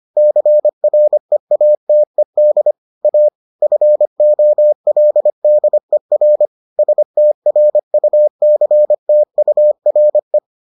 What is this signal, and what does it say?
Morse code, heard as a keyed tone, about 25 wpm, 600 Hz. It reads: CREATED A FOLDER STRUCTURE